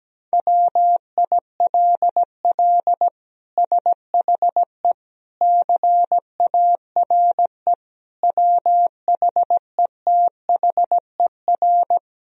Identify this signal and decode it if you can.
Morse code, heard as a keyed tone, about 17 wpm, 700 Hz. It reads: WILL SHE CARE WHETHER
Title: Sentences from Top 1000 Words - Review Part 3 - 17wpm